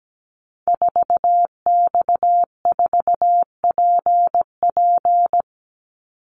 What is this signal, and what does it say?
Morse code, heard as a keyed tone, about 17 wpm, 700 Hz. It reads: 4X4PP
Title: Call Signs - 17wpm